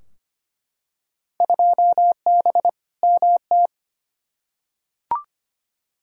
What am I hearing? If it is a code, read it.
Morse code, heard as a keyed tone, about 25 wpm, 700 Hz. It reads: 2B MT
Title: Field Day Exchanges Encore - 25wpm